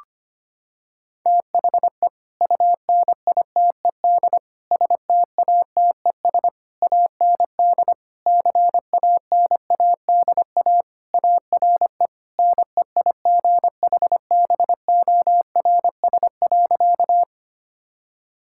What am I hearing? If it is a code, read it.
Morse code, heard as a keyed tone, about 25 wpm, 700 Hz. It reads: THE UNITED STATES AND CANADA ARE NEIGHBORS.
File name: practice / 25wpm / Sentences with 3-Word Phrases - 25wpm